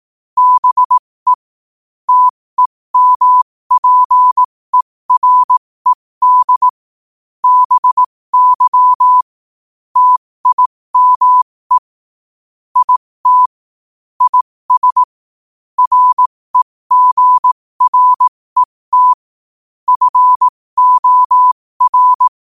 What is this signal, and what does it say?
Morse code, heard as a keyed tone, about 18 wpm, spaced as for 15 wpm, 1000 Hz. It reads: BE TEMPERED BY TIME IT IS REGRET FOR
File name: qotdfiles / 15wpm